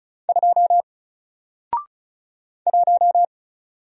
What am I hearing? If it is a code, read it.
Morse code, heard as a keyed tone, about 36 wpm, 700 Hz. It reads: 2 1
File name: Single-Letter-Number-35wpm